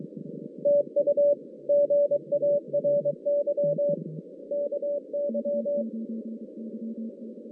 Morse code, heard as a keyed tone, about 23 wpm, 580 Hz. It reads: TU GARY KY
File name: 2016 JAN CW NAQP / 2016 Jan naqp k5ot - 0-9